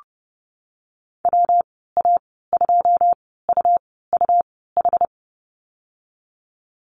Morse code, heard as a keyed tone, about 30 wpm, spaced as for 17 wpm, 700 Hz. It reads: WA2UUH
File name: Lesson 19g - Letter w - Call Signs 30wpm